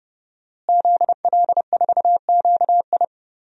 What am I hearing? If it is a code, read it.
Morse code, heard as a keyed tone, about 30 wpm, 700 Hz. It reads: ZL4QI
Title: DXCC-100 Call Signs - 5 Characters - Repeated 3x - 30wpm